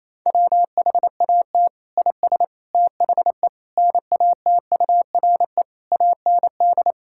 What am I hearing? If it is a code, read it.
Morse code, heard as a keyed tone, about 28 wpm, 700 Hz. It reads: WHAT IS THE NATURE AND
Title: Sentences from Top 1000 Words - Review Part 5 - 28wpm